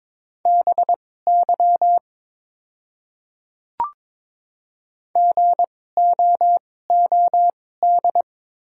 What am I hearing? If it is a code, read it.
Morse code, heard as a keyed tone, about 22 wpm, spaced as for 16 wpm, 700 Hz. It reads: BY GOOD
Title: 2x Character Spacing - Top 300 Words - 22wpm